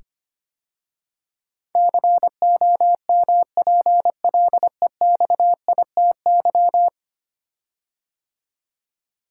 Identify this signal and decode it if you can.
Morse code, heard as a keyed tone, about 25 wpm, 700 Hz. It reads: COMPLEXITY
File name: 10 Letter Words - First 3 Letters Given - 25wpm